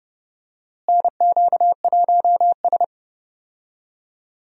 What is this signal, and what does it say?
Morse code, heard as a keyed tone, about 30 wpm, 700 Hz. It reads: NQ1S
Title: USA Call Signs - 4 Characters - Repeated 2x - 30wpm